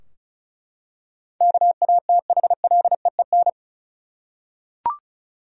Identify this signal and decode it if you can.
Morse code, heard as a keyed tone, about 36 wpm, 700 Hz. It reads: KATHLEEN
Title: USA Common Names - 35wpm